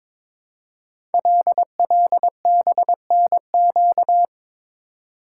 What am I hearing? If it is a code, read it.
Morse code, heard as a keyed tone, about 22 wpm, 700 Hz. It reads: LLBNQ
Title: Code Groups - 5 Letters - 22wpm